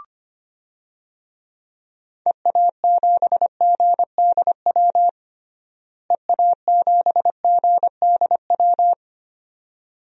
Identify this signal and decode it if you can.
Morse code, heard as a keyed tone, about 25 wpm, 700 Hz. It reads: EA7GDW EA7GDW
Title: DXCC-100 Call Signs - 6 Characters - Repeated 2x - 25wpm